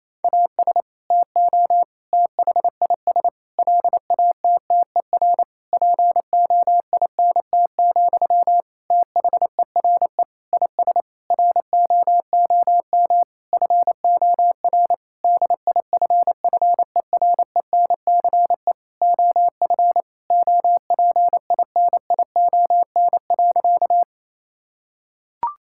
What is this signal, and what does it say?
Morse code, heard as a keyed tone, about 28 wpm, 700 Hz. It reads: AS TO THIS LATTER POINT, THERE IS ROOM FOR DIFFERENCE OF OPINION.
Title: Sentences from Top 1600 Words 28wpm